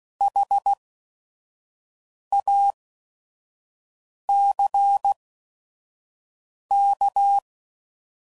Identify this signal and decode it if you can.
Morse code, heard as a keyed tone, about 16 wpm, spaced as for 5 wpm, 800 Hz. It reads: HACK
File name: Prefixes / By First Letter / Hs